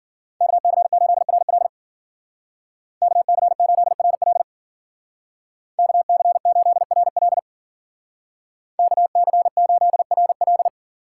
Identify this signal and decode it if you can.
Morse code, heard as a keyed tone, about 48 wpm, 700 Hz. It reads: KC8RL KC8RL KC8RL KC8RL
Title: Speed Racing - USA 2x2 Call Signs 40wpm